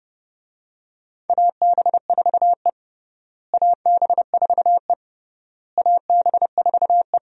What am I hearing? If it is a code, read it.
Morse code, heard as a keyed tone, about 30 wpm, 700 Hz. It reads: AB4E AB4E AB4E
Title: USA Call Signs - 4 Characters - Repeated 3x - 30wpm